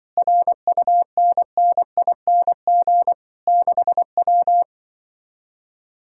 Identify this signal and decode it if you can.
Morse code, heard as a keyed tone, about 24 wpm, 700 Hz. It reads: RUNNING 6W
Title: Speed-Racing Common QSO Phrases - 20wpm